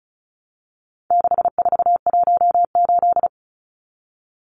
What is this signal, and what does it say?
Morse code, heard as a keyed tone, about 34 wpm, 700 Hz. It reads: 6418
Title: Numbers - 4-Digit 35wpm